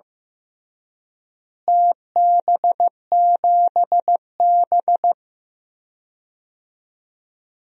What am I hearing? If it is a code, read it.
Morse code, heard as a keyed tone, about 15 wpm, 700 Hz. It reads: TB7B
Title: DXCC-100 Call Signs - 4 Characters - Repeated 2x - 15wpm